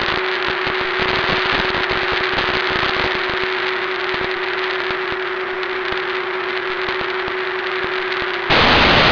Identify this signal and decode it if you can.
PSK31, about 31.25 baud.
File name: BRICSAT PSK31 W3ADO-6 C bnaaatvmaaaaaaaaewdBdz